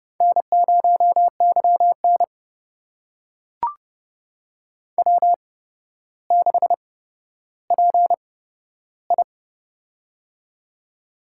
Morse code, heard as a keyed tone, about 30 wpm, spaced as for 13 wpm, 700 Hz. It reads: N0YN W6PI
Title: USA Call Signs - 4 Characters - 8x Character Spacing - 30wpm